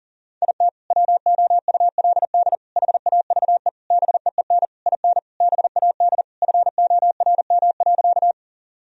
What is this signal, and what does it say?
Morse code, heard as a keyed tone, about 40 wpm, 700 Hz. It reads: IT WOULD HAVE BEEN IN BAD FORM.
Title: Sentences from Top 1000 Words - Review Part 1 - 40wpm